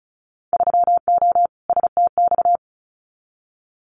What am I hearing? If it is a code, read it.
Morse code, heard as a keyed tone, about 36 wpm, 690 Hz. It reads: 3O STX
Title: Winter Field Day Exchanges 35wpm